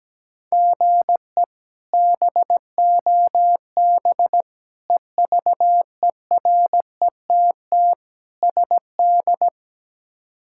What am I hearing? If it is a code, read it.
Morse code, heard as a keyed tone, about 17 wpm, 700 Hz. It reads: GE BOB EVERETT SD
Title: K1USN SST QSO Practice - 17wpm